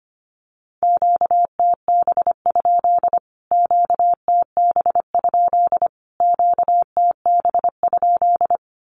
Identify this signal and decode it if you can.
Morse code, heard as a keyed tone, about 25 wpm, 700 Hz. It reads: QTB? QTB? QTB?
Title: Q-Codes Repeated 3x 25wpm